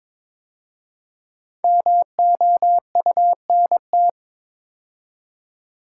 Morse code, heard as a keyed tone, about 22 wpm, 690 Hz. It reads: MOUNT
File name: 5 Letter Words - First 3 Letters Given - 22wpm